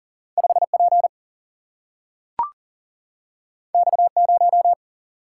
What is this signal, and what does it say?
Morse code, heard as a keyed tone, about 40 wpm, 700 Hz. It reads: 5 P X 0
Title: Single-Letter Single-Number 40wpm